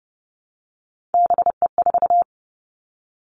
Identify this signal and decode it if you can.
Morse code, heard as a keyed tone, about 30 wpm, 700 Hz. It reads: BE4
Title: Three Challenging Letter-Number Combinations 30wpm